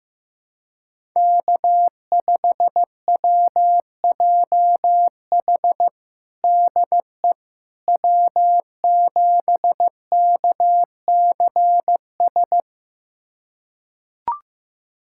Morse code, heard as a keyed tone, about 15 wpm, 700 Hz. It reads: K5WJH DE W7KCS
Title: USA Callsign DE Callsign - 5 Characters - 15wpm